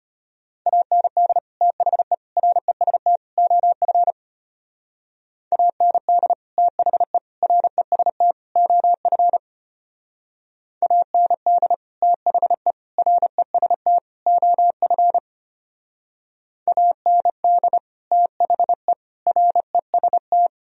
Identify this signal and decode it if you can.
Morse code, heard as a keyed tone, about 32 wpm, 700 Hz. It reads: AND THE REST OF AND THE REST OF AND THE REST OF AND THE REST